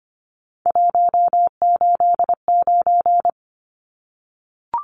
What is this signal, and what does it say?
Morse code, heard as a keyed tone, about 25 wpm, 700 Hz. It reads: 189
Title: Numbers - 3-Digit 25wpm